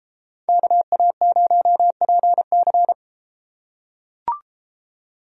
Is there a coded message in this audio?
Yes: Morse code from 0.5 to 2.9 s, heard as a keyed tone, about 33 wpm, 700 Hz. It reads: KA0PC